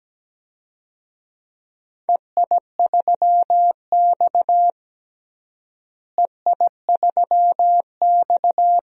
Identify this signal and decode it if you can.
Morse code, heard as a keyed tone, about 17 wpm, 700 Hz. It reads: EI3X EI3X
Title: DXCC-100 Call Signs - 4 Characters - Repeated 2x - 17wpm